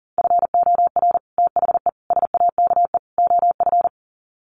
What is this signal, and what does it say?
Morse code, heard as a keyed tone, about 40 wpm, 700 Hz. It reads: FOR THE SAKE OF
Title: Sets of 4 Words 40wpm